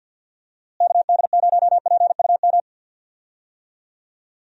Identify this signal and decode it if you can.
Morse code, heard as a keyed tone, about 51 wpm, 700 Hz. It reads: KD0PUM
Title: USA Call Signs - 6 Characters - Repeated 3x - 50wpm